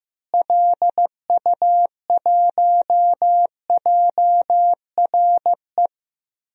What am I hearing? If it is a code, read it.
Morse code, heard as a keyed tone, about 14 wpm, 700 Hz. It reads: LU1JRE
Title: DXCC-100 Call Signs - 6 Characters - Repeated 3x - 15wpm